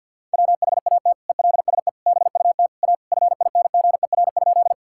Morse code, heard as a keyed tone, about 50 wpm, 700 Hz. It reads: WHAT ELSE BUT A FINGER?
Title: Sentences from Top 1600 Words 50wpm